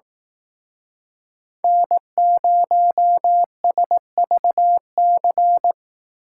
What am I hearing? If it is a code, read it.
Morse code, heard as a keyed tone, about 18 wpm, 700 Hz. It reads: N0SVC
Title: Speed Racing - USA 1x3 Call Signs 15wpm